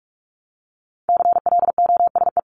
Morse code, heard as a keyed tone, about 44 wpm, 700 Hz. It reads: CLOSE
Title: LY Suffix - Top 100 Words with Base Words 45wpm